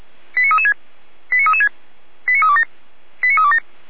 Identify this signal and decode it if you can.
Five-tone selective call (ZVEI), heard as quick runs of five tones, tones of about 70 ms. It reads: 89336 89336 89327 89327